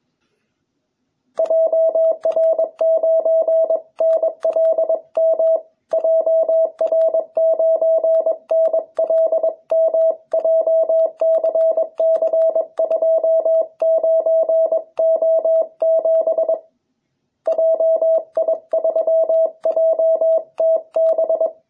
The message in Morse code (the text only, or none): JR9NLM JR9NLM JCC29O7 JI3JTB